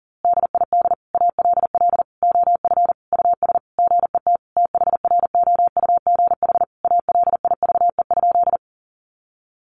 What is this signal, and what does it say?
Morse code, heard as a keyed tone, about 40 wpm, 700 Hz. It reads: DID ALL OF US GET THROUGH ALIVE?
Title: A Sound Of Thunder 40wpm